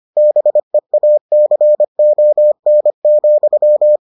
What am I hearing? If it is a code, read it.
Morse code, heard as a keyed tone, about 25 wpm, 600 Hz. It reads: BEACON,